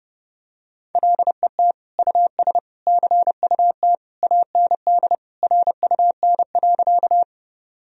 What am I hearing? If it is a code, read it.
Morse code, heard as a keyed tone, about 30 wpm, 700 Hz. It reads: LET US CUT AND RUN.
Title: Sentences from Top 1000 Words - Review Part 1 - 30wpm